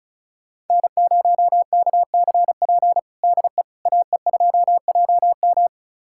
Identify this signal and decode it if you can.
Morse code, heard as a keyed tone, about 36 wpm, 690 Hz. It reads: N0KCP DE AE2JM
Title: USA Callsign DE Callsign 35wpm